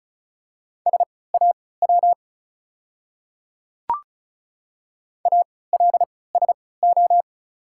Morse code, heard as a keyed tone, about 36 wpm, spaced as for 20 wpm, 700 Hz. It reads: SAW ALSO
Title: 3x Character Spacing - Top 300 Words - 35wpm